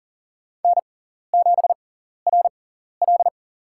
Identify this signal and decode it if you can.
Morse code, heard as a keyed tone, about 40 wpm, spaced as for 14 wpm, 700 Hz. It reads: N7RL
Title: USA Call Signs - 6x Character Spacing - 40wpm